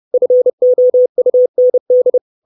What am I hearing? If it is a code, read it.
Morse code, heard as a keyed tone, about 30 wpm, 500 Hz. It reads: FOUND
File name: 632 Words 30WPM